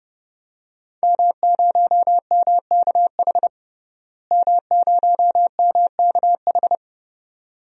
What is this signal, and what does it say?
Morse code, heard as a keyed tone, about 30 wpm, 700 Hz. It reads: M0MKH M0MKH
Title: DXCC-100 Call Signs - 5 Characters - Repeated 2x - 30wpm